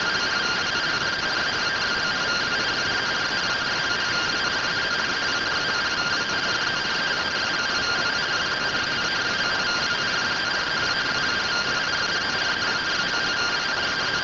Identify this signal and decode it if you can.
Two-tone teleprinter signal, about 170 Hz shift.
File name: findings / CT AXIAL -111.gen-31 rtty rx